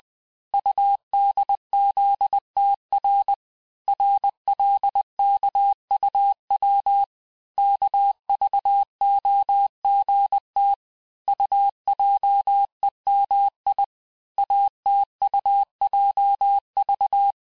Morse code, heard as a keyed tone, about 20 wpm, 800 Hz. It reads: UDZTR RLKUW KVOGT UJEMI ATUJV
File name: cours-858-texte-chiffre-vitesse-19mpm